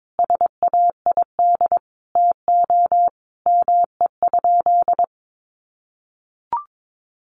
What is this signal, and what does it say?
Morse code, heard as a keyed tone, about 22 wpm, 700 Hz. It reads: SAID TO ME?
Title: Sentences from Top 200 Words 22wpm